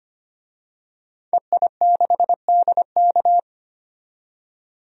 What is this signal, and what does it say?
Morse code, heard as a keyed tone, about 25 wpm, 700 Hz. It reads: EI6DK
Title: DXCC-100 Call Signs - 5 Characters - 25wpm